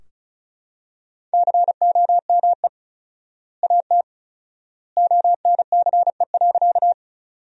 Morse code, heard as a keyed tone, about 34 wpm, 700 Hz. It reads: COME AT ONCE.
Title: 4x Word-Spacing - Sentences from Top 300 Words - 35wpm